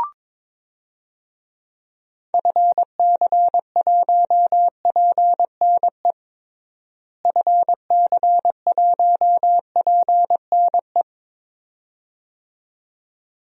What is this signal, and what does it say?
Morse code, heard as a keyed tone, about 22 wpm, 700 Hz. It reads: FC1PNE FC1PNE
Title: DXCC-100 Call Signs - 6 Characters - Repeated 2x - 22wpm